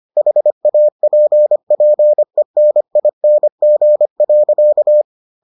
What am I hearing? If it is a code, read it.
Morse code, heard as a keyed tone, about 25 wpm, 600 Hz. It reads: HAPPENING.